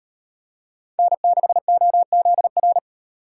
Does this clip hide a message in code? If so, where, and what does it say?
Yes: Morse code from 1.0 to 2.8 s, heard as a keyed tone, about 36 wpm, 700 Hz. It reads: N6OZR